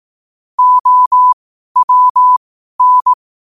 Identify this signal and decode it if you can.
Morse code, heard as a keyed tone, about 18 wpm, spaced as for 12 wpm, 1000 Hz. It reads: OWN
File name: qotdfiles / 12wpm